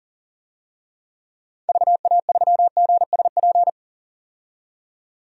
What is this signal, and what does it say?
Morse code, heard as a keyed tone, about 40 wpm, 700 Hz. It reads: VA3GSP
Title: DXCC-100 Call Signs - 6 Characters - 40wpm